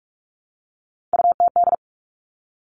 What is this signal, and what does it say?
Morse code, heard as a keyed tone, about 46 wpm, 700 Hz. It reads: UTD
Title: Three Challenging Letter-Number Combinations 45wpm